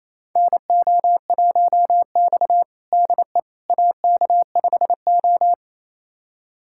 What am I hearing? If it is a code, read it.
Morse code, heard as a keyed tone, about 27 wpm, 690 Hz. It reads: NO1X DE AK5O
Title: USA Callsign DE Callsign - 4 Characters - 28wpm